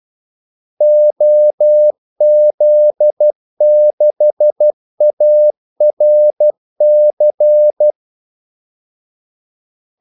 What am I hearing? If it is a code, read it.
Morse code, heard as a keyed tone, about 12 wpm, 600 Hz. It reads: OZ6ARC
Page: Kaldesignaler 12 wpm